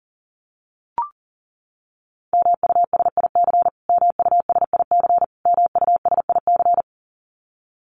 Morse code, heard as a keyed tone, about 37 wpm, 700 Hz. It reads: MUSIC MUSIC MUSIC
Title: Top 300 Words Repeated 40wpm